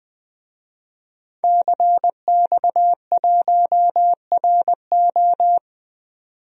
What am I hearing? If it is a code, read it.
Morse code, heard as a keyed tone, about 20 wpm, 700 Hz. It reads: CX1RO
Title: DXCC-100 Call Signs - 5 Characters - 20wpm